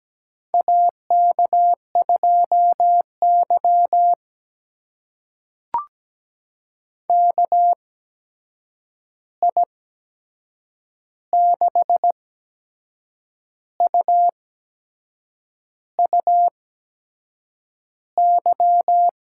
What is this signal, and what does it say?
Morse code, heard as a keyed tone, about 17 wpm, spaced as for 6 wpm, 700 Hz. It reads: AK2Y KI6UUY
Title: USA Call Signs - 8x Character Spacing - 17wpm